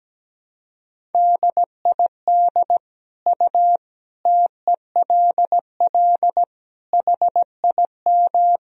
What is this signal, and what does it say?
Morse code, heard as a keyed tone, about 17 wpm, 700 Hz. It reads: DID U TELL HIM
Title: Sentences from Top 400 Words and Abbreviations 17wpm